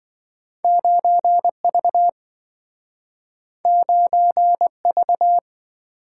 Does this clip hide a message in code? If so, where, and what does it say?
Yes: Morse code from 0.6 to 5.4 s, heard as a keyed tone, about 27 wpm, 700 Hz. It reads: 9V 9V